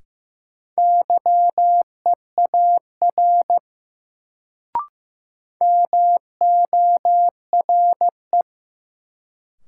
Morse code, heard as a keyed tone, about 15 wpm, 700 Hz. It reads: YEAR MORE
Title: Sets of 2 Words - New Word Drill - 15wpm